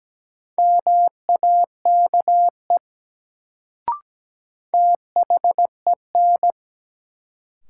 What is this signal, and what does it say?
Morse code, heard as a keyed tone, about 17 wpm, 700 Hz. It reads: MAKE THEN
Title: Sentences from Top 100 Words - New Word Drill - 17wpm